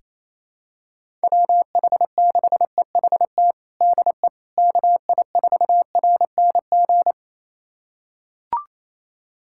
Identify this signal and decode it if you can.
Morse code, heard as a keyed tone, about 28 wpm, 700 Hz. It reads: WH6EHT DE KI4RNG
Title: USA Callsign DE Callsign 28wpm